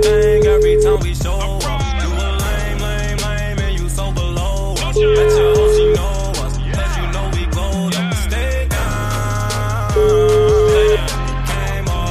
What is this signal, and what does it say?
Ringback tone (Europe, 425 Hz), with noise.